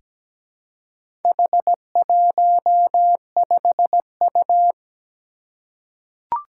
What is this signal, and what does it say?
Morse code, heard as a keyed tone, about 17 wpm, 700 Hz. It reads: H15U
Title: Four Challenging Letter-Number Combinations 17wpm